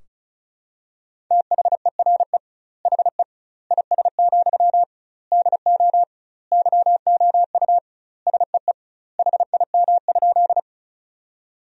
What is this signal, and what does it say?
Morse code, heard as a keyed tone, about 36 wpm, 700 Hz. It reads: THERE HE IS, DO YOU SEE HIM?
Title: 2x Word-Spacing - Sentences from Top 100 Words - 35wpm